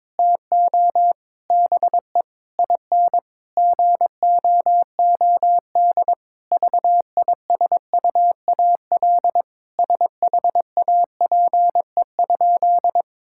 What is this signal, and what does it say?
Morse code, heard as a keyed tone, about 21 wpm, 700 Hz. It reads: TO BE IN GOOD VISUAL SHAPE?
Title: Sentences from Top 1000 Words - Review Part 6 - 22wpm